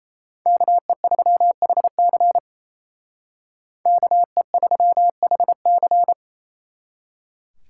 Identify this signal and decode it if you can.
Morse code, heard as a keyed tone, about 32 wpm, 700 Hz. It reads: KE3HC KE3HC
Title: Speed Racing - USA 2x2 Call Signs 28wpm